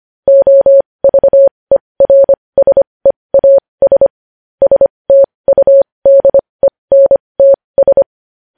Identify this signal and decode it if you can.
Morse code, heard as a keyed tone, about 25 wpm, spaced as for 20 wpm, 550 Hz. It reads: OVERSEAS STUDENTS
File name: podcast / News headlines-1x-25x20